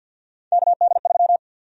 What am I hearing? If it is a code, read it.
Morse code, heard as a keyed tone, about 48 wpm, 700 Hz. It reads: KD3
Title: USA Call Sign Prefixes - 50wpm